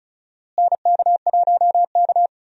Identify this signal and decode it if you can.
Morse code, heard as a keyed tone, about 34 wpm, 700 Hz. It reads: NK1K
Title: USA Call Signs - 4 Characters - Repeated 2x - 35wpm